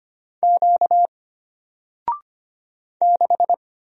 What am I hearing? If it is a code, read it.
Morse code, heard as a keyed tone, about 25 wpm, 700 Hz. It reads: Q 6
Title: Single Letter-Number 25wpm